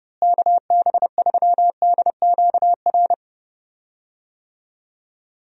Morse code, heard as a keyed tone, about 30 wpm, 700 Hz. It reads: KB3DQR
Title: USA Call Signs - 6 Characters - Repeated 3x - 30wpm